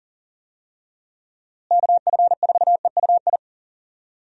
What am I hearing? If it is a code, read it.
Morse code, heard as a keyed tone, about 40 wpm, 700 Hz. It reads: KF4EUI
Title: USA Call Signs - 6 Characters - Repeated 2x - 40wpm